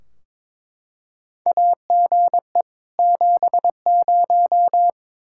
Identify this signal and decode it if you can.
Morse code, heard as a keyed tone, about 22 wpm, 700 Hz. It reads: AGE 70
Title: Practice QSO - Exchange 3 - More Info - 22wpm